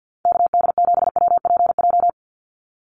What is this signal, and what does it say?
Morse code, heard as a keyed tone, about 50 wpm, 700 Hz. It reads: KD7WPJ
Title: NA SOTA Activator Callsigns 50wpm